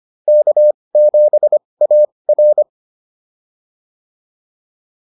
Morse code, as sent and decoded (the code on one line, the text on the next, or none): -.- --... .- .-.
K7AR